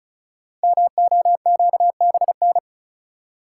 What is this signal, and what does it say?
Morse code, heard as a keyed tone, about 36 wpm, 700 Hz. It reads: MOQBN
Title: Code Groups - 5 Letters - 35wpm